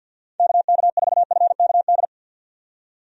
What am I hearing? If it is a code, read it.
Morse code, heard as a keyed tone, about 49 wpm, 700 Hz. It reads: KK4FKD
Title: USA Call Signs - 6 Characters - 50wpm